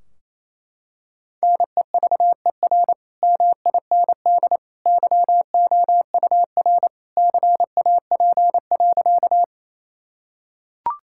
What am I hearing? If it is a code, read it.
Morse code, heard as a keyed tone, about 28 wpm, 700 Hz. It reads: NEVER MIND YOUR CAP.
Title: Sentences from Top 1600 Words 28wpm